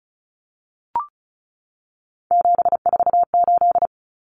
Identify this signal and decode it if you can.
Morse code, heard as a keyed tone, about 34 wpm, 700 Hz. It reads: 748
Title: Common Cut Numbers - 3-Digit 35wpm